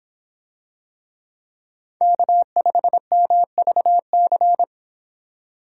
Morse code, heard as a keyed tone, about 24 wpm, 700 Hz. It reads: K5MVC